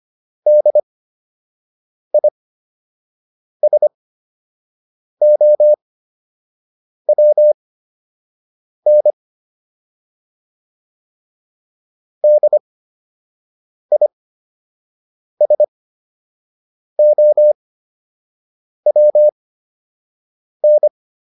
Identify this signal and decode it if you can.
Morse code, heard as a keyed tone, about 25 wpm, spaced as for 6 wpm, 600 Hz. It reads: DISOWN DISOWN